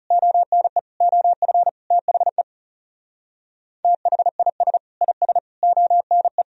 Morse code, heard as a keyed tone, about 34 wpm, 700 Hz. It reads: ONE OF THE THIS IS ONE
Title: Speed-Racing - Sets of 5 Words 30wpm